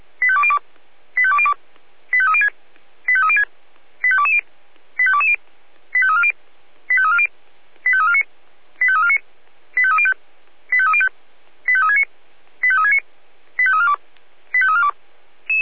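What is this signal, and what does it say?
Five-tone selective call (ZVEI), heard as quick runs of five tones, tones of about 70 ms. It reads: 85202 85202 85306 85306 85220 85220 85340 85340 85349 85349 85305 85305 85369 85369 85342 85342
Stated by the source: ZVEI five-tone call received by SDR.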